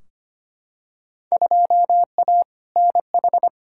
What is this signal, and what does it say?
Morse code, heard as a keyed tone, about 25 wpm, 700 Hz. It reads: 2A NH
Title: Field Day Exchanges Encore - 25wpm